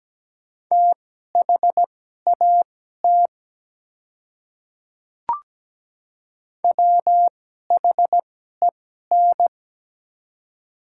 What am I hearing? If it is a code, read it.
Morse code, heard as a keyed tone, about 17 wpm, spaced as for 12 wpm, 690 Hz. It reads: THAT WHEN
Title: 2x Character Spacing - Top 100 Words - 17wpm